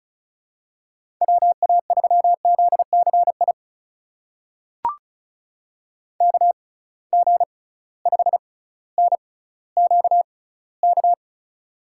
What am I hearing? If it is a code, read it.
Morse code, heard as a keyed tone, about 36 wpm, spaced as for 18 wpm, 700 Hz. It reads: WA3ZCI KG5NQK
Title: USA Call Signs - 6 Characters - 6x Character Spacing - 35wpm